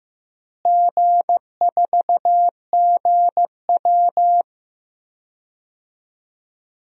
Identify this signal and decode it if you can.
Morse code, heard as a keyed tone, about 15 wpm, 700 Hz. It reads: G4GW
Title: DXCC-100 Call Signs - 4 Characters - Repeated 3x - 15wpm